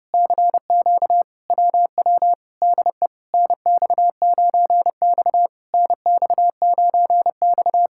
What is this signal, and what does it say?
Morse code, heard as a keyed tone, about 29 wpm, 700 Hz. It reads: CQ WW DE NX9X NX9X